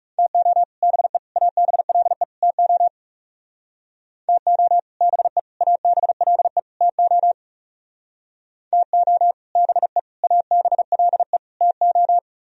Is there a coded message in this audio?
Yes: Morse code from 0.2 to 12.2 s, heard as a keyed tone, about 38 wpm, 700 Hz. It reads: TO BE ABLE TO TO BE ABLE TO TO BE ABLE TO